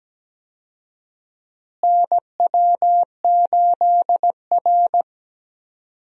Morse code, heard as a keyed tone, about 17 wpm, 700 Hz. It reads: NW8R
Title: USA Call Signs - 4 Characters - Repeated 2x - 17wpm